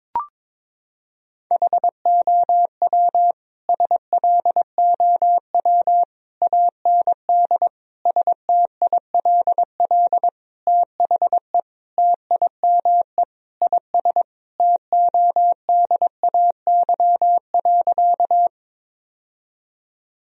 Morse code, heard as a keyed tone, about 22 wpm, 700 Hz. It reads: HOW SLOW AND STILL THE TIME IS TODAY.
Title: Sentences from Top 1000 Words - Review Part 2 - 22wpm